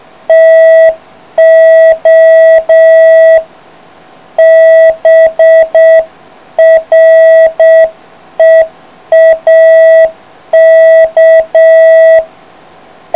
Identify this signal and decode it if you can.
Morse code, heard as a keyed tone, about 6 wpm, 660 Hz. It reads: TO BREAK